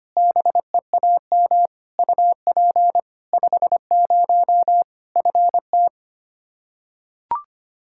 Morse code, heard as a keyed tone, about 25 wpm, 700 Hz. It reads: BEAM UP 50 FT
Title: Practice QSO - Exchange 2 - Info - 25wpm